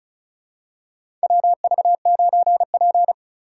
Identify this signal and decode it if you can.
Morse code, heard as a keyed tone, about 36 wpm, 700 Hz. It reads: WV9P
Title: USA Call Signs - 4 Characters - 4x Character Spacing - 35wpm